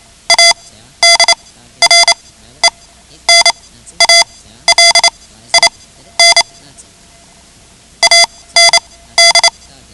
Morse code, heard as a keyed tone, about 28 wpm, spaced as for 14 wpm, 2050 Hz. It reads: ADRENALIN AND
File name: ham radio / CodeQuick / Code Quick 10